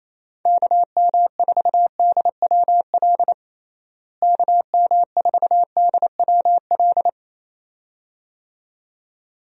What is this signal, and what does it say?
Morse code, heard as a keyed tone, about 28 wpm, 700 Hz. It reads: KM4DWL KM4DWL
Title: USA Call Signs - 6 Characters - Repeated 2x - 28wpm